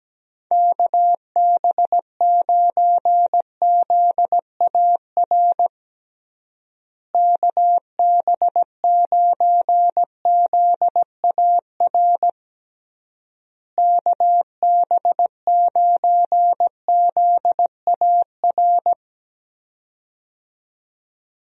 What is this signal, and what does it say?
Morse code, heard as a keyed tone, about 17 wpm, 700 Hz. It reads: KB9ZAR KB9ZAR KB9ZAR
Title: USA Call Signs - 6 Characters - Repeated 3x - 17wpm